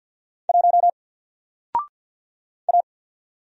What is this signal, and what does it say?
Morse code, heard as a keyed tone, about 52 wpm, 700 Hz. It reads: 1 A
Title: Single Letter-Number Alt 50wpm